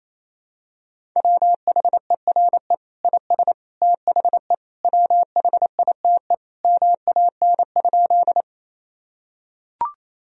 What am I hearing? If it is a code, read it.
Morse code, heard as a keyed tone, about 27 wpm, 700 Hz. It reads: WHERE IS THE WHITE MAN?
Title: Sentences from Top 1000 Words - Review Part 1 - 28wpm